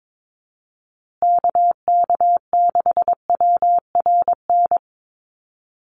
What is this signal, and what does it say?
Morse code, heard as a keyed tone, about 22 wpm, 700 Hz. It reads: KK6WRN
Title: USA Call Signs Repeated 2x 22wpm